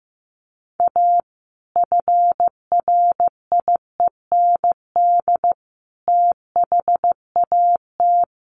Morse code, heard as a keyed tone, about 15 wpm, 700 Hz. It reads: A FRIEND THAT
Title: Sentences from Top 1500 Words 15wpm